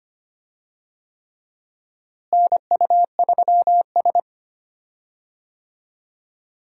Morse code, heard as a keyed tone, about 25 wpm, 700 Hz. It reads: NU3S
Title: USA Call Signs - 4 Characters - Repeated 3x - 25wpm